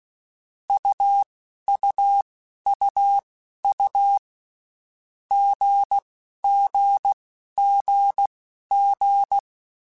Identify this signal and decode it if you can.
Morse code, heard as a keyed tone, about 16 wpm, spaced as for 11 wpm, 800 Hz. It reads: UUUU GGGG
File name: Basic Letter Groups By7Groups / AWUNDG / AWUNDG@10